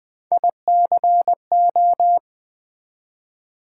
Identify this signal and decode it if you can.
Morse code, heard as a keyed tone, about 20 wpm, 700 Hz. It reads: ICO
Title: Build A Word - 1 Letter At A Time - 3 to 5-Letter Words - 20wpm